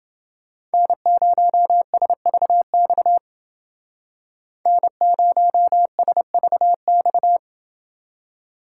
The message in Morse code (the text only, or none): N0SVX N0SVX